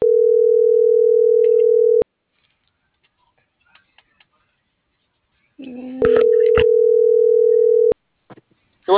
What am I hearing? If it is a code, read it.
Ringback tone (North America), 440 + 480 Hz.